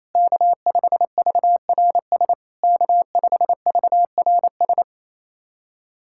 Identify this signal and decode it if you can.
Morse code, heard as a keyed tone, about 28 wpm, 700 Hz. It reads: K5VRS K5VRS
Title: SOTA QSO Practice 28wpm